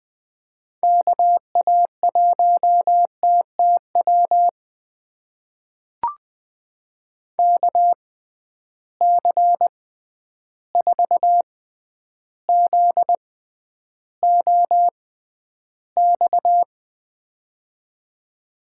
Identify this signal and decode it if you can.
Morse code, heard as a keyed tone, about 20 wpm, spaced as for 10 wpm, 700 Hz. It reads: KA1TTW KC4ZOX
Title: USA Call Signs - 6 Characters - 6x Character Spacing - 20wpm